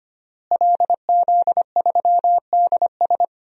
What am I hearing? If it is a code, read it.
Morse code, heard as a keyed tone, about 25 wpm, 700 Hz. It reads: LZ3DS
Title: DXCC-100 Call Signs - 5 Characters - Repeated 2x - 25wpm